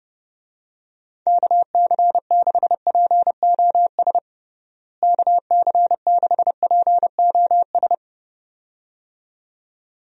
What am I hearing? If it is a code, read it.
Morse code, heard as a keyed tone, about 30 wpm, 700 Hz. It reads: KC6POS KC6POS
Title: USA Call Signs - 6 Characters - Repeated 2x - 30wpm